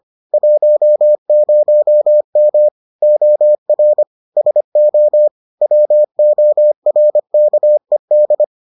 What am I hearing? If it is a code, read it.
Morse code, heard as a keyed tone, about 25 wpm, 600 Hz. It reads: 10M OR SO WORKED